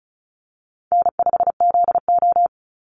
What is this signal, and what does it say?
Morse code, heard as a keyed tone, about 34 wpm, 700 Hz. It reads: N5ZO
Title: Call Signs - 35wpm